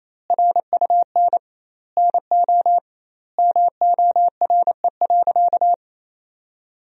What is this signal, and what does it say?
Morse code, heard as a keyed tone, about 28 wpm, 700 Hz. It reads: RUN NO MORE.
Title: 2x Word-Spacing - Sentences from Top 300 Words - 28wpm